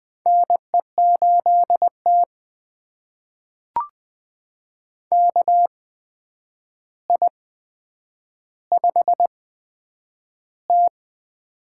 Morse code, heard as a keyed tone, about 20 wpm, spaced as for 9 wpm, 700 Hz. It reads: NE8T KI5T
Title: USA Call Signs - 4 Characters - 8x Character Spacing - 20wpm